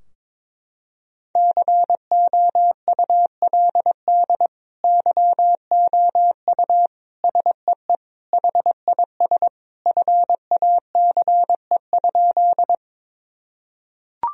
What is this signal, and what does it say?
Morse code, heard as a keyed tone, about 22 wpm, 700 Hz. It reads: COULD YOU SEE HIS FACE?
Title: Sentences from Top 1000 Words - Review Part 2 - 22wpm